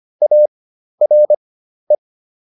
Morse code, heard as a keyed tone, about 25 wpm, spaced as for 12 wpm, 600 Hz. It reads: ARE